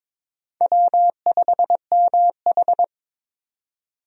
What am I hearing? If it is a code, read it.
Morse code, heard as a keyed tone, about 22 wpm, 700 Hz. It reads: W5MH
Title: USA Call Signs - 4 Characters - 22wpm